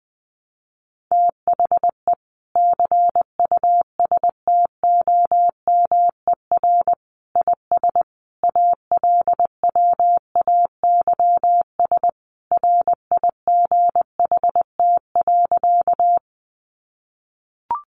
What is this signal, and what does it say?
Morse code, heard as a keyed tone, about 20 wpm, 700 Hz. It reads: THE CUSTOMER IS ALWAYS RIGHT.
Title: Sentences from Top 1100 Words 20wpm